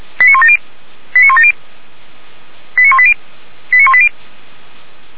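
Five-tone selective call (ZVEI), heard as quick runs of five tones, tones of about 70 ms. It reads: 89170 89170 89180 89180
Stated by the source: ZVEI five-tone selective call 89138 received via SDR, from Feuerwehr Orlamünde audio archive.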